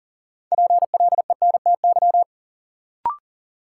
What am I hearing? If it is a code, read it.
Morse code, heard as a keyed tone, about 37 wpm, 700 Hz. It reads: PLENTY
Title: Sentences from Top 1300 Words - New Word Drill Alt - 40wpm